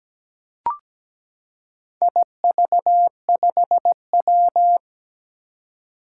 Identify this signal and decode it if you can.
Morse code, heard as a keyed tone, about 17 wpm, 700 Hz. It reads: IV5W
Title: Four Challenging Letter-Number Combinations 17wpm